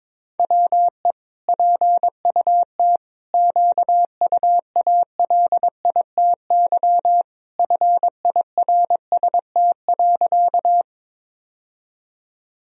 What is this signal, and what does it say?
Morse code, heard as a keyed tone, about 21 wpm, 700 Hz. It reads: WE PUT QUALITY FIRST.
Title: Sentences from Top 1000 Words - Review Part 4 - 22wpm